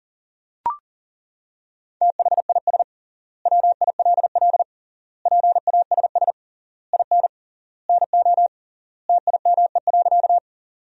Morse code, heard as a keyed tone, about 40 wpm, 700 Hz. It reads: THIS WILL PASS IN NO TIME.
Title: 3x Word-Spacing - Sentences from Top 400 Words - 40wpm